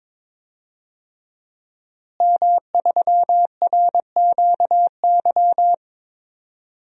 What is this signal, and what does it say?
Morse code, heard as a keyed tone, about 22 wpm, 700 Hz. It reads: M3RQY
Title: DXCC-100 Call Signs - 5 Characters - Repeated 2x - 22wpm